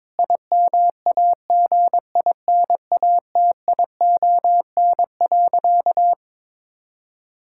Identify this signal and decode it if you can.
Morse code, heard as a keyed tone, about 22 wpm, 700 Hz. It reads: IMAGINATION.
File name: Difficult Sentences with Hints - 22wpm